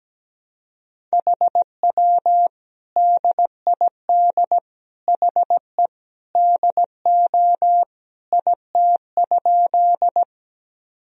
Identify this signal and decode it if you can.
Morse code, heard as a keyed tone, about 17 wpm, 700 Hz. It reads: HW DID HE DO IT?
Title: Sentences from Top 100 Words and Abbreviations 17wpm